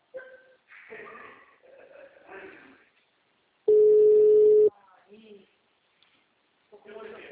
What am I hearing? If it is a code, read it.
Ringback tone (Europe, 425 Hz), with speech.